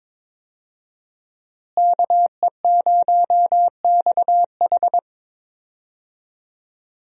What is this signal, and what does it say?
Morse code, heard as a keyed tone, about 22 wpm, 700 Hz. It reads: KE0XH
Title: USA Call Signs - 5 Characters - Repeated 2x - 22wpm